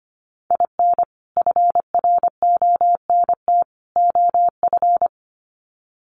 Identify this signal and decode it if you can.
Morse code, heard as a keyed tone, about 24 wpm, 700 Hz. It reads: IN FRONT OF
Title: Sets of 3 Words 25wpm